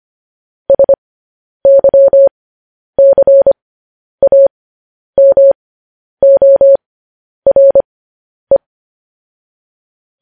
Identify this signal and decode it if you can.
Morse code, heard as a keyed tone, about 25 wpm, spaced as for 10 wpm, 550 Hz. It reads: SYCAMORE
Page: news headlines in morse code at 10 wpm